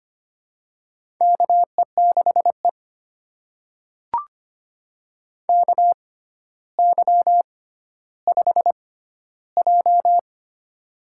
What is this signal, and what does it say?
Morse code, heard as a keyed tone, about 25 wpm, spaced as for 13 wpm, 700 Hz. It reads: KE6E KY5J
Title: USA Call Signs - 4 Characters - 6x Character Spacing - 25wpm